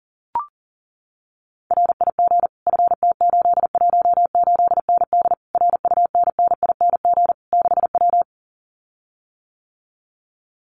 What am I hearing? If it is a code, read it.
Morse code, heard as a keyed tone, about 40 wpm, 700 Hz. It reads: RIG FT818ND RUNNING 6W
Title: Common QSO Phrases Encore 40wpm